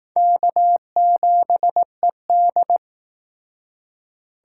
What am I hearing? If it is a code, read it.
Morse code, heard as a keyed tone, about 19 wpm, 700 Hz. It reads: K7ED
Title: Speed Racing - USA 1x2 Call Signs 15wpm